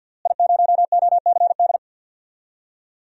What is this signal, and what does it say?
Morse code, heard as a keyed tone, about 52 wpm, 700 Hz. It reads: I0OCD
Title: DXCC-100 Call Signs - 5 Characters - Repeated 2x - 50wpm